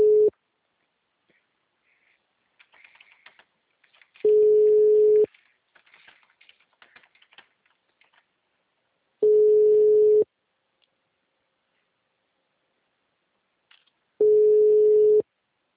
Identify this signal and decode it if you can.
Ringback tone (Europe, 425 Hz).
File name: out- -2012- - - .36